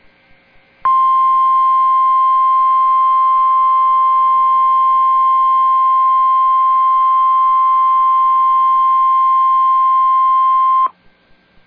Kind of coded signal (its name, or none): Emergency Alert System (SAME) tones